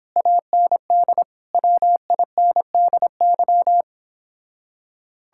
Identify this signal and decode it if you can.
Morse code, heard as a keyed tone, about 29 wpm, 700 Hz. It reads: AND WINDY